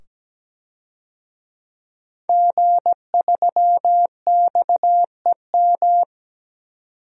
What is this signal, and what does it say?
Morse code, heard as a keyed tone, about 17 wpm, 700 Hz. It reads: G3XEM
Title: DXCC-100 Call Signs - 5 Characters - 17wpm